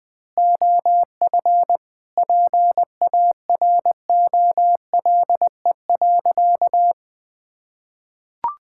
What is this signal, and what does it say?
Morse code, heard as a keyed tone, about 20 wpm, 700 Hz. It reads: OF PAROLE.
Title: Sentences with 8-Word Phrases - Spoken Beginning - 20wpm